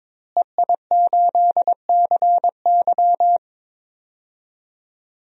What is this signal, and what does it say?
Morse code, heard as a keyed tone, about 22 wpm, 700 Hz. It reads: EI8CY
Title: DXCC-100 Call Signs - 5 Characters - Repeated 3x - 22wpm